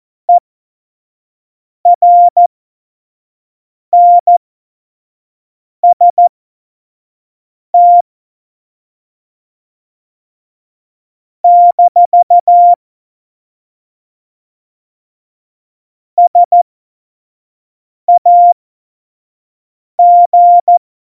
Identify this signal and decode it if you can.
Morse code, heard as a keyed tone, about 14 wpm, spaced as for 5 wpm, 700 Hz. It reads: ERNST - SAG
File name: Morsekurs / Witze / Witz 19. Dezember 2017, 25 F70